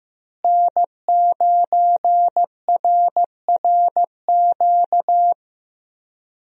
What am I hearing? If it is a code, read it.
Morse code, heard as a keyed tone, about 15 wpm, 700 Hz. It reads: N9RRQ
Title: USA Call Signs - 5 Characters - Repeated 2x - 15wpm